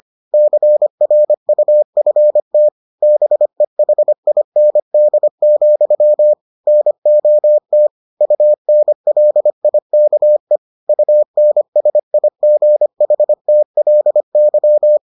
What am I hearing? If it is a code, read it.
Morse code, heard as a keyed tone, about 25 wpm, 600 Hz. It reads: CRUFT BEHIND, NOT UNLIKE UNSIGHTLY